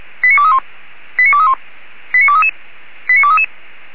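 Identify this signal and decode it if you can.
Five-tone selective call (ZVEI), heard as quick runs of five tones, tones of about 70 ms. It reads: 89231 89231 89233 89233
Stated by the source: ZVEI five-tone selective call for address 89240, received by SDR.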